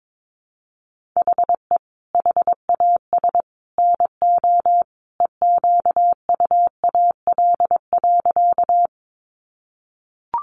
Morse code, heard as a keyed tone, about 22 wpm, 700 Hz. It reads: HE HAS NO EQUAL.
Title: Sentences from Top 1400 Words 22wpm